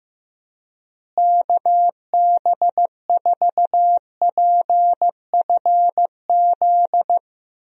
Morse code, heard as a keyed tone, about 15 wpm, 700 Hz. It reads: KB4PFZ
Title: USA Call Signs - 6 Characters - Repeated 2x - 15wpm